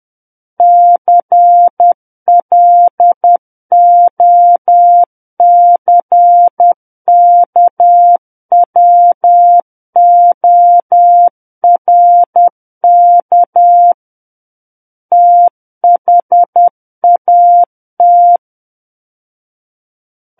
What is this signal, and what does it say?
Morse code, heard as a keyed tone, about 10 wpm, 700 Hz. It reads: CLOCKWORK THAT
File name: Book One / 01-Chapter 1 - 10 WPM